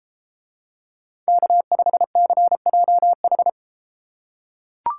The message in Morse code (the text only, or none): K5CJH